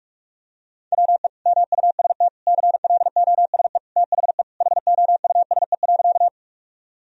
Morse code, heard as a keyed tone, about 46 wpm, 700 Hz. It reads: WE MUST CLOSE THE HOUSE.
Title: Sentences from Top 1000 Words - Review Part 1 - 45wpm